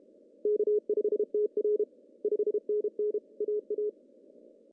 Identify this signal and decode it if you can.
Morse code, heard as a keyed tone, about 32 wpm, 410 Hz. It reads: K5TR 5NN AA
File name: CW / cqww-cw-k5tr